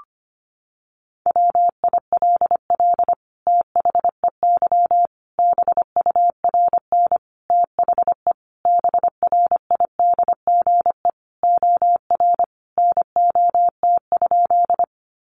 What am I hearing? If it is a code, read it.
Morse code, heard as a keyed tone, about 25 wpm, 700 Hz. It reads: WILL THEY BURN THE BRIDGE OR NOT?
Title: Sentences from Top 1300 Words 25wpm